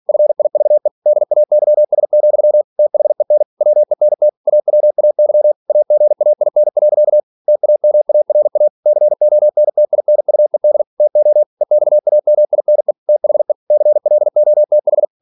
Morse code, heard as a keyed tone, about 46 wpm, 600 Hz. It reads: FIVE DAYS, THEN WENT AWAY AGAIN. TAMARA CONTINUED TO EXAMINE THE CLOTH
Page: Never 47wpm